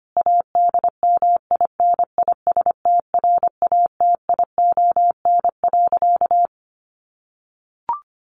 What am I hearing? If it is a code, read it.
Morse code, heard as a keyed tone, about 25 wpm, 700 Hz. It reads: ADMINISTRATION.
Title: Sentences from Top 1500 Words 25wpm